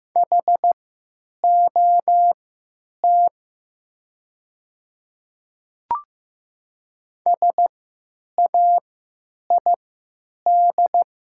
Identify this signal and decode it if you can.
Morse code, heard as a keyed tone, about 15 wpm, spaced as for 9 wpm, 700 Hz. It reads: HOT SAID
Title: 3x Character Spacing - Top 100 Words - 15wpm